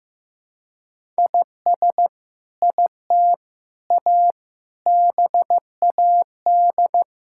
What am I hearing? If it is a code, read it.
Morse code, heard as a keyed tone, about 15 wpm, 700 Hz. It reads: IS IT A BAD
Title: Sentences from Top 1000 Words - Review Part 5 - 15wpm